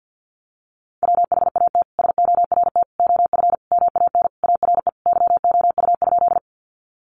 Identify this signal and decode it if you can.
Morse code, heard as a keyed tone, about 48 wpm, 700 Hz. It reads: WHAT SORT OF MAN ARE YOU?
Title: Sentences from Top 1300 Words 50wpm